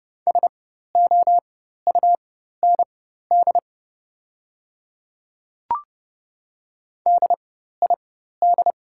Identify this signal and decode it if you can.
Morse code, heard as a keyed tone, about 30 wpm, spaced as for 14 wpm, 700 Hz. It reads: SOUND DID
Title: 4x Character Spacing - Top 100 Words - 30wpm